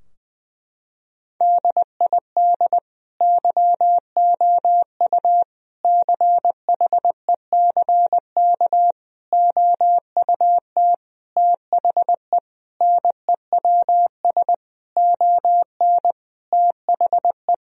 Morse code, heard as a keyed tone, about 20 wpm, 700 Hz. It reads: DID YOU CHECK OUT THE NEWS ON THE
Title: Sentences from Top 1000 Words - Review Part 4 - 20wpm